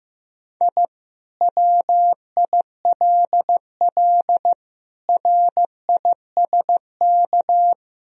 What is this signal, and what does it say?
Morse code, heard as a keyed tone, about 15 wpm, 700 Hz. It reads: I WILL RISK
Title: Sentences from Top 1000 Words - Review Part 4 - 15wpm